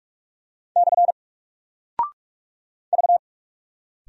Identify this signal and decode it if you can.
Morse code, heard as a keyed tone, about 48 wpm, 700 Hz. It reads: / V
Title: Single Character (Letter:Number:Punctuation:Prosign) Alt 45wpm